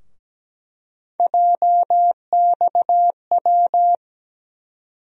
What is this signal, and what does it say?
Morse code, heard as a keyed tone, about 17 wpm, 700 Hz. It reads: JXW
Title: USA Call Sign Postfixes - 3 Letters 17wpm